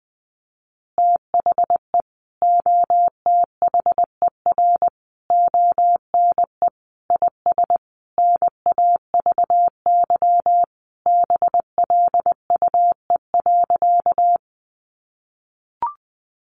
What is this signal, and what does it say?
Morse code, heard as a keyed tone, about 20 wpm, 700 Hz. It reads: THE OTHER ONE IS NAVY BLUE.
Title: Additional Easy Phrases 20wpm